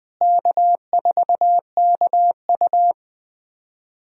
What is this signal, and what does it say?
Morse code, heard as a keyed tone, about 20 wpm, 700 Hz. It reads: K4KU
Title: USA Call Signs - 4 Characters - Repeated 3x - 20wpm